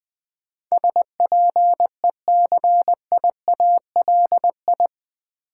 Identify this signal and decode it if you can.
Morse code, heard as a keyed tone, about 21 wpm, 700 Hz. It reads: SPECIALI
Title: Build A Word - 4 Letters At A Time - 11-Letter Words - 20wpm